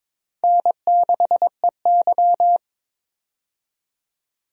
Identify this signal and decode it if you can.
Morse code, heard as a keyed tone, about 22 wpm, 700 Hz. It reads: N6EY
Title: USA Call Signs - 4 Characters - 22wpm